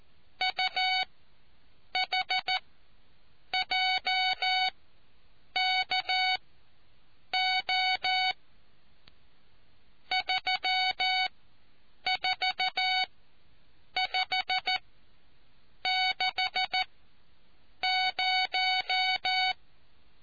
Morse code, heard as a keyed tone, about 14 wpm, spaced as for 7 wpm, 750 Hz. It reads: UHJKO 34560